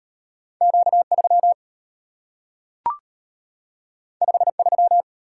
Morse code, heard as a keyed tone, about 35 wpm, 700 Hz. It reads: Q3 53